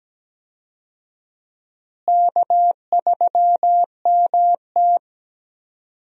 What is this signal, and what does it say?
Morse code, heard as a keyed tone, about 17 wpm, 700 Hz. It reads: K3MT
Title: USA Call Signs - 4 Characters - Repeated 2x - 17wpm